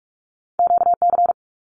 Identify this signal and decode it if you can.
Morse code, heard as a keyed tone, about 46 wpm, 700 Hz. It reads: QC
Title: Canadian Provinces and Territories 45wpm